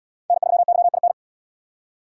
Morse code, heard as a keyed tone, about 71 wpm, 700 Hz. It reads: N3YIN